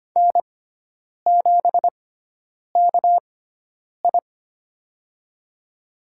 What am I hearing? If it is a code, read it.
Morse code, heard as a keyed tone, about 25 wpm, spaced as for 9 wpm, 700 Hz. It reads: N7KI
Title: USA Call Signs - 4 Characters - 6x Character Spacing - 25wpm